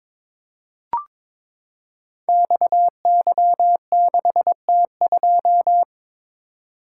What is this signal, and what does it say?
Morse code, heard as a keyed tone, about 22 wpm, 700 Hz. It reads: XY6T2
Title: Code Groups - 5 Letter:Numbers - 22wpm